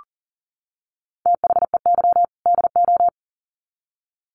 Morse code, heard as a keyed tone, about 40 wpm, 700 Hz. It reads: THEY DO
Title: Sets of 2 Words 40wpm